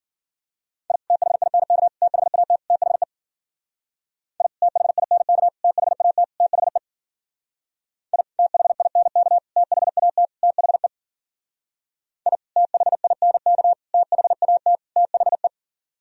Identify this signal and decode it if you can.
Morse code, heard as a keyed tone, about 48 wpm, 700 Hz. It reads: I THINK THAT THE I THINK THAT THE I THINK THAT THE I THINK THAT THE
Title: Speed-Racing - Sets of 4 Words 40wpm